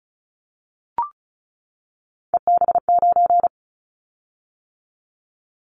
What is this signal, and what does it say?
Morse code, heard as a keyed tone, about 34 wpm, 700 Hz. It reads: EB9
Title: Callsign Prefixes for Top 100 DXCC - 35wpm